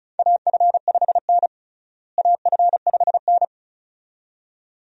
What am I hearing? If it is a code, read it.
Morse code, heard as a keyed tone, about 36 wpm, 700 Hz. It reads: AF5N AF5N
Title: USA Call Signs - 4 Characters - Repeated 2x - 35wpm